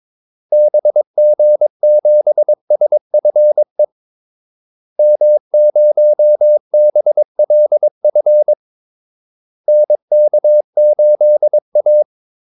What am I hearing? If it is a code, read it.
Morse code, heard as a keyed tone, about 22 wpm, 600 Hz. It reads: BG7SFE M0BLF NK8A
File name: Callsigns-22wpm